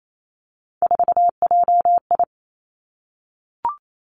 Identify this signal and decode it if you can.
Morse code, heard as a keyed tone, about 28 wpm, 700 Hz. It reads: 4JI
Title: Three Challenging Letter-Number Combinations 28wpm